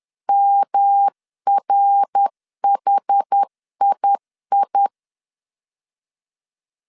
Morse code, heard as a keyed tone, about 11 wpm, 800 Hz. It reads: MRHII
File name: 10wpm